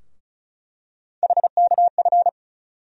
Morse code, heard as a keyed tone, about 36 wpm, 700 Hz. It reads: HKF
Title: USA Call Sign Postfixes - 3 Letters 35wpm